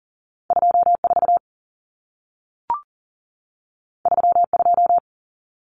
Morse code, heard as a keyed tone, about 40 wpm, 700 Hz. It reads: 24 32
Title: Numbers - 2-Digit 40wpm